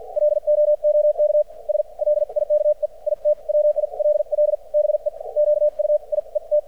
Morse code, heard as a keyed tone, about 50 wpm, spaced as for 40 wpm, 600 Hz. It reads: LOOKS LIKE IT WILL BE QUIET
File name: archived / CRSnet